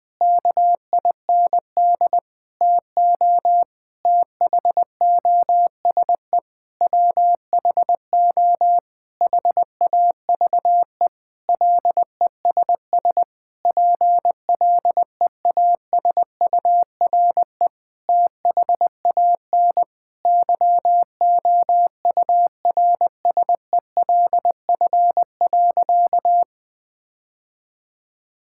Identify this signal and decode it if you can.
Morse code, heard as a keyed tone, about 20 wpm, 700 Hz. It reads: KIND TO THOSE WHO HAVE LESS PLEASURE THAN YOURSELF.
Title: Sentences from Top 1600 Words 20wpm